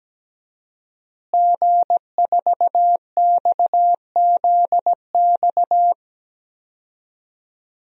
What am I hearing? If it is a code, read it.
Morse code, heard as a keyed tone, about 17 wpm, 700 Hz. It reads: G4XZX
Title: DXCC-100 Call Signs - 5 Characters - Repeated 3x - 17wpm